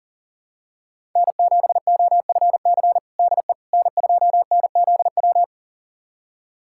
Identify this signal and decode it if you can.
Morse code, heard as a keyed tone, about 37 wpm, 700 Hz. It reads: N7OFC DE N2NZW
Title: USA Callsign DE Callsign - 5 Characters - 40wpm